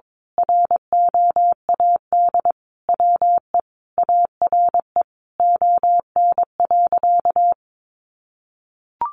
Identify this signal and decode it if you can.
Morse code, heard as a keyed tone, about 22 wpm, 700 Hz. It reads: ROAD WE ARE ON.
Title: Sentences from Top 1100 Words 22wpm